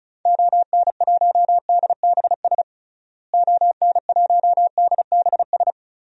Morse code, heard as a keyed tone, about 36 wpm, 700 Hz. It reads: ON1DBS ON1DBS
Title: DXCC-100 Call Signs - 6 Characters - Repeated 2x - 35wpm